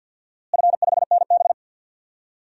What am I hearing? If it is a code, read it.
Morse code, heard as a keyed tone, about 47 wpm, 700 Hz. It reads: F5NB
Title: DXCC-100 Call Signs - 4 Characters - Repeated 2x - 50wpm